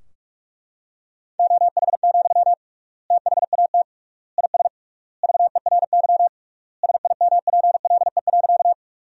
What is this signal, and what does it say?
Morse code, heard as a keyed tone, about 42 wpm, 700 Hz. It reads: OH, THAT IS VERY SIMPLE.
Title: 3x Word-Spacing - Sentences from Top 500 Words - 45wpm